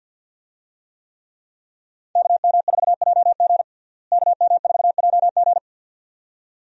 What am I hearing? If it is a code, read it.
Morse code, heard as a keyed tone, about 49 wpm, 700 Hz. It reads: KM4JG KM4JG
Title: USA Call Signs - 5 Characters - Repeated 2x - 50wpm